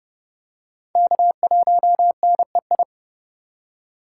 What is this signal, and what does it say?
Morse code, heard as a keyed tone, about 29 wpm, 700 Hz. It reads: K1NEI